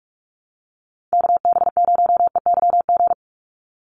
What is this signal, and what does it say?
Morse code, heard as a keyed tone, about 45 wpm, 700 Hz. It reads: KB0EYG
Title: USA Call Signs 45wpm